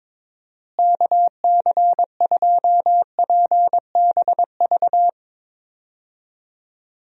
Morse code, heard as a keyed tone, about 22 wpm, 700 Hz. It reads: KC2PBV
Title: USA Call Signs - 6 Characters - Repeated 3x - 22wpm